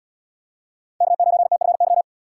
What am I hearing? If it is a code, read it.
Morse code, heard as a keyed tone, about 76 wpm, 700 Hz. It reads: K9TFY